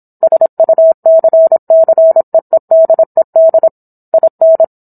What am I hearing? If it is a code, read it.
Morse code, heard as a keyed tone, about 27 wpm, 650 Hz. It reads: SUCCEEDED IN